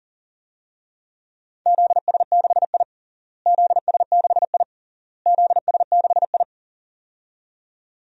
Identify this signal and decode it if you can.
Morse code, heard as a keyed tone, about 40 wpm, 700 Hz. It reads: ZS6I ZS6I ZS6I
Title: DXCC-100 Call Signs - 4 Characters - Repeated 3x - 40wpm